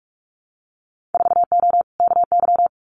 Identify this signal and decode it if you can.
Morse code, heard as a keyed tone, about 46 wpm, 700 Hz. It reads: 4O KY
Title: Winter Field Day Exchanges Encore 45wpm